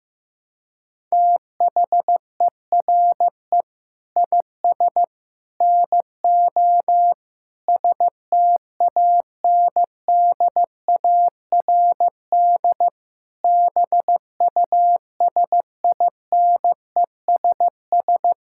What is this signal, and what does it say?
Morse code, heard as a keyed tone, about 15 wpm, 700 Hz. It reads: THERE IS NO STANDARD BUSINESS
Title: Sentences from Top 1000 Words - Review Part 6 - 15wpm